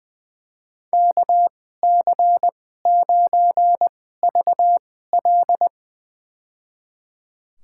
Morse code, heard as a keyed tone, about 20 wpm, spaced as for 14 wpm, 700 Hz. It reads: KC9VL
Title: USA Call Signs - 5 Characters - 2x Character Spacing - 20wpm